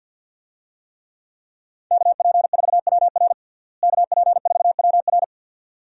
Morse code, heard as a keyed tone, about 52 wpm, 700 Hz. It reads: KP4WR KP4WR
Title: USA Call Signs - 5 Characters - Repeated 2x - 50wpm